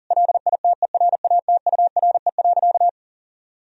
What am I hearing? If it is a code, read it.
Morse code, heard as a keyed tone, about 40 wpm, 700 Hz. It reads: LITERATURE.
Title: Sentences from Top 1000 Words - Review Part 3 - 40wpm